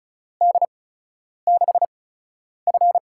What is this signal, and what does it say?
Morse code, heard as a keyed tone, about 36 wpm, spaced as for 10 wpm, 700 Hz. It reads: D6F
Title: USA Call Signs - 8x Character Spacing - 35wpm